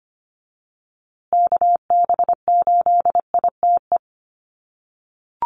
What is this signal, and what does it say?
Morse code, heard as a keyed tone, about 25 wpm, 700 Hz. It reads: KB8ITE
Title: USA Call Signs Repeated 2x 25wpm